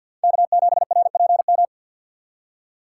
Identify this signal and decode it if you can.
Morse code, heard as a keyed tone, about 53 wpm, 700 Hz. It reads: K7RPM
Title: USA Call Signs - 4x Character Spacing - 50wpm